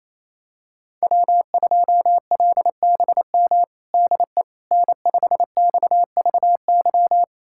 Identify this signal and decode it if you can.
Morse code, heard as a keyed tone, about 28 wpm, 700 Hz. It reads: W2LBM DE N5XVY
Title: USA Callsign DE Callsign 28wpm